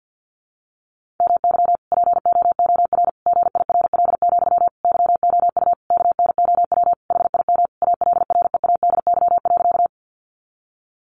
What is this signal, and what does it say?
Morse code, heard as a keyed tone, about 48 wpm, 700 Hz. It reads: MY POOR GIRL, YOU KNOW HIM ALREADY.
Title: Sentences from Top 1000 Words 50wpm